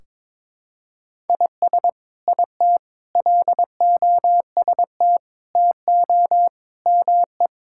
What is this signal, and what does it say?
Morse code, heard as a keyed tone, about 22 wpm, 700 Hz. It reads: IS IT LOST TO ME
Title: Sentences from Top 1600 Words 22wpm